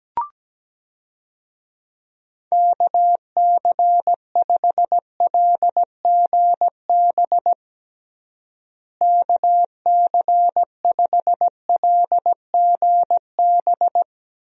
Morse code, heard as a keyed tone, about 17 wpm, 700 Hz. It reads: KC5LGB KC5LGB
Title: USA Call Signs - 6 Characters - Repeated 3x - 17wpm